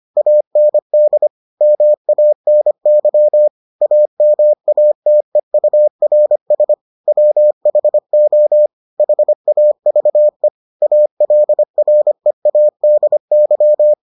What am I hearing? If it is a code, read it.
Morse code, heard as a keyed tone, about 25 wpm, 600 Hz. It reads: AND MANY AMATEURS WHO HAVE ALREADY